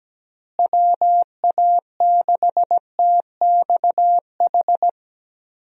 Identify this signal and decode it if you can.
Morse code, heard as a keyed tone, about 17 wpm, 700 Hz. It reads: WA6TXH
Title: USA Call Signs - 6 Characters - Repeated 2x - 17wpm